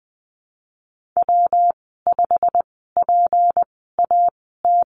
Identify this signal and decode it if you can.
Morse code, heard as a keyed tone, about 20 wpm, spaced as for 15 wpm, 700 Hz. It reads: W5PAT
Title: 186 Lesson 30g - Letter b - Call Signs 20wpm